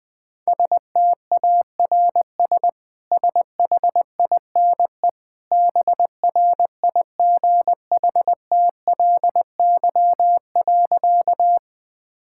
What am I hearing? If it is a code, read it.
Morse code, heard as a keyed tone, about 21 wpm, 700 Hz. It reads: STARS SHINE BRIGHTLY.
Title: Easy Sentences with Hints - 20wpm